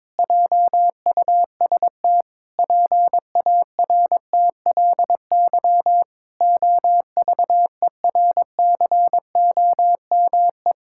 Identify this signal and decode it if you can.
Morse code, heard as a keyed tone, about 22 wpm, 700 Hz. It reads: JUST PARTLY OVERCOME
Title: Pangram Challenge v2 - 22wpm